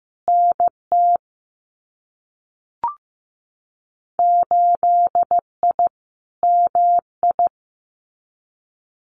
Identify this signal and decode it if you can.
Morse code, heard as a keyed tone, about 15 wpm, 700 Hz. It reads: NT 8I MI
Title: Winter Field Day Exchanges Encore 15wpm